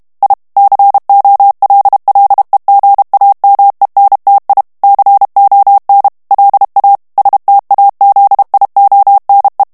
Morse code, heard as a keyed tone, about 32 wpm, 800 Hz. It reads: I COLLEGAMENTI CON LA STAZIONE
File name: ITALIEN texte clair 1 vitesse 30 mots minute